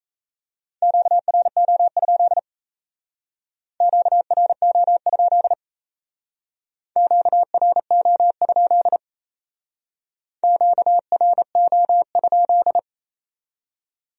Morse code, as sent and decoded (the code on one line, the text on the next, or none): --.- .-. --- ..--.. / --.- .-. --- ..--.. / --.- .-. --- ..--.. / --.- .-. --- ..--..
QRO? QRO? QRO? QRO?